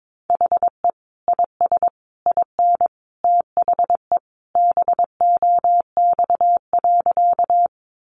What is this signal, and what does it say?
Morse code, heard as a keyed tone, about 22 wpm, 700 Hz. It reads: HE IS IN THE BOX.
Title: Sentences from Top 500 Words 22wpm